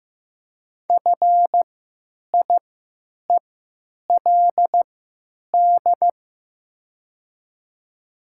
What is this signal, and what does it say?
Morse code, heard as a keyed tone, about 15 wpm, spaced as for 9 wpm, 700 Hz. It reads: FIELD
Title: 3x Character Spacing - Top 500 Words - 15wpm